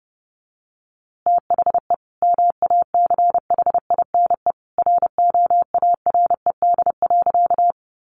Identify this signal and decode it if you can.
Morse code, heard as a keyed tone, about 30 wpm, 700 Hz. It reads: THE MACHINE ROARED.
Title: A Sound Of Thunder 30wpm